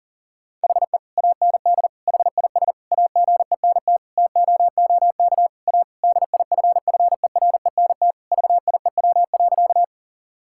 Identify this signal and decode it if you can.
Morse code, heard as a keyed tone, about 40 wpm, 700 Hz. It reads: HE AND HIS AGENT TOOK A DIFFERENT VIEW.
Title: Sentences from Top 1600 Words 40wpm